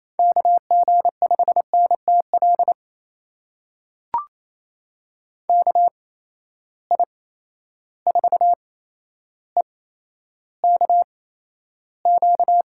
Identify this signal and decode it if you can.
Morse code, heard as a keyed tone, about 28 wpm, spaced as for 12 wpm, 700 Hz. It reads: KG5NTL KI4EKQ
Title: USA Call Signs - 6 Characters - 8x Character Spacing - 28wpm